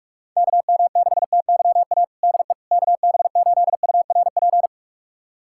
Morse code, heard as a keyed tone, about 46 wpm, 700 Hz. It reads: KM6TYA DE KB8URP
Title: USA Callsign DE Callsign 45wpm